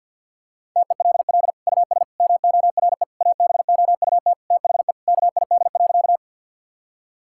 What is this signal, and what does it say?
Morse code, heard as a keyed tone, about 51 wpm, 700 Hz. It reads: TELL US MORE ABOUT THE KID.
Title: Sentences from Top 1600 Words 50wpm